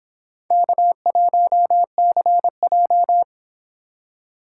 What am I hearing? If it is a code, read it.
Morse code, heard as a keyed tone, about 26 wpm, 700 Hz. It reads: K1CJ
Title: Speed Racing - USA Call Signs 22wpm